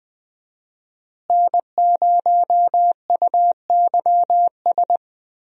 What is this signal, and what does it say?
Morse code, heard as a keyed tone, about 20 wpm, 700 Hz. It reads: N0UYS
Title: USA Call Signs - 5 Characters - Repeated 3x - 20wpm